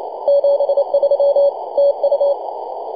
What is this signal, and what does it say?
Morse code, heard as a keyed tone, about 27 wpm, 560 Hz, with noise. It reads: 73 TU
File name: posts / 2015-dec-05-0655z-e51xgi-10-115mhz-cw